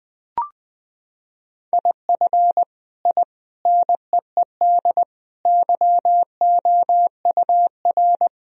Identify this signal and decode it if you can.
Morse code, heard as a keyed tone, about 20 wpm, 700 Hz. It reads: IF I NEED YOUR
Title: Sentences from Top 1000 Words - Review Part 1 - 20wpm